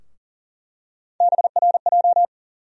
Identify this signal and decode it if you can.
Morse code, heard as a keyed tone, about 40 wpm, 700 Hz. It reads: BRJ
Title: USA Call Sign Postfixes - 3 Letters 40wpm